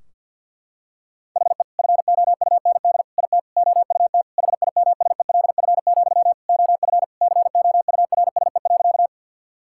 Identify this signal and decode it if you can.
Morse code, heard as a keyed tone, about 52 wpm, 700 Hz. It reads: HE FOUND IT OUT HIMSELF, OF COURSE.
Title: Sentences from Top 1600 Words 50wpm